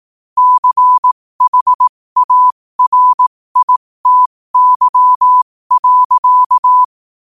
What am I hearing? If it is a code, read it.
Morse code, heard as a keyed tone, about 18 wpm, spaced as for 15 wpm, 1000 Hz. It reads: CHARITY.
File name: qotdfiles / 15wpm